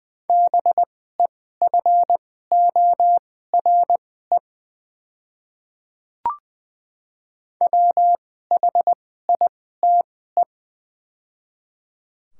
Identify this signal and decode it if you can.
Morse code, heard as a keyed tone, about 20 wpm, spaced as for 14 wpm, 700 Hz. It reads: BEFORE WHITE
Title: 2x Character Spacing - Top 300 Words - 20wpm